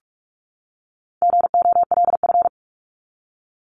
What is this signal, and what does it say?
Morse code, heard as a keyed tone, about 44 wpm, 700 Hz. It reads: GOLF
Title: Top 1751-2000 Words 45wpm